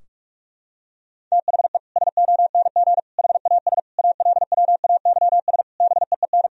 Morse code, heard as a keyed tone, about 46 wpm, 700 Hz. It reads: THE SONG HAS ALWAYS BEEN
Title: Sentences from Top 1000 Words - Review Part 3 - 45wpm